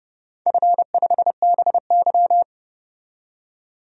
Morse code, heard as a keyed tone, about 30 wpm, 700 Hz. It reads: F5BY
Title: DXCC-100 Call Signs - 4 Characters - 30wpm